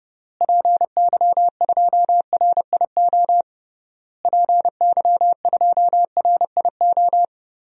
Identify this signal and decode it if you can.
Morse code, heard as a keyed tone, about 32 wpm, 700 Hz. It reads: PY2RIO PY2RIO
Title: DXCC-100 Call Signs - 6 Characters - Repeated 3x - 30wpm